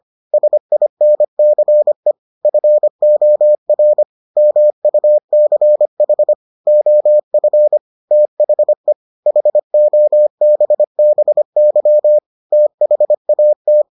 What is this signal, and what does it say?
Morse code, heard as a keyed tone, about 25 wpm, 600 Hz. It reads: SINCE FOR MUCH OF THE HOBBY THAT